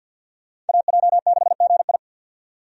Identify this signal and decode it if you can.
Morse code, heard as a keyed tone, about 47 wpm, 700 Hz. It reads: AJ6GI
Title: USA Call Signs - 5 Characters - 50wpm